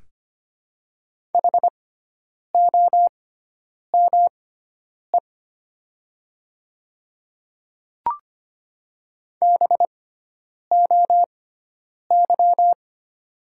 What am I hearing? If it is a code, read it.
Morse code, heard as a keyed tone, about 25 wpm, spaced as for 9 wpm, 700 Hz. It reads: HOME BOY
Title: 6x Character Spacing - Top 200 Words - 25wpm